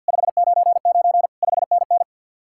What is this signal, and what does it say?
Morse code, heard as a keyed tone, about 65 wpm, spaced as for 57 wpm, 700 Hz. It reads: 599 5NN
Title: Speed-Racing Common QSO Phrases - 50wpm